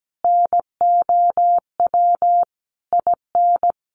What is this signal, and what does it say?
Morse code, heard as a keyed tone, about 17 wpm, 700 Hz. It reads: NOW IN
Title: Sentences from Top 1300 Words 17wpm